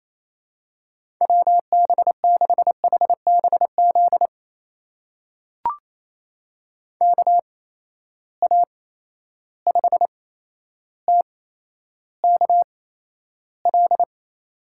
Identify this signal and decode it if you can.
Morse code, heard as a keyed tone, about 28 wpm, spaced as for 12 wpm, 700 Hz. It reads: WB6HBZ KA5TKL
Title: USA Call Signs - 6 Characters - 8x Character Spacing - 28wpm